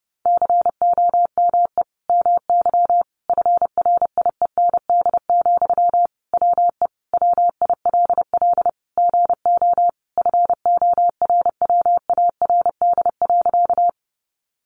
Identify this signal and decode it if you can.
Morse code, heard as a keyed tone, about 30 wpm, 700 Hz. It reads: COME MY FRIEND, WE WILL GO FORWARD.
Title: Sentences from Top 1300 Words 30wpm